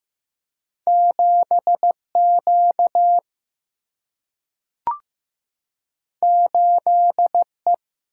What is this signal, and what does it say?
Morse code, heard as a keyed tone, about 15 wpm, 700 Hz. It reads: 7 Q 8 E
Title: Single-Letter Single-Number 15wpm